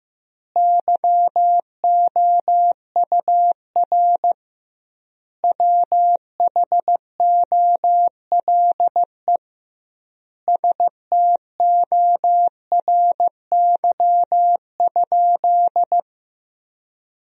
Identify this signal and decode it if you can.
Morse code, heard as a keyed tone, about 15 wpm, 700 Hz. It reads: YOUR WHOLE STORY?
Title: 2x Word-Spacing - Sentences from Top 400 Words - 15wpm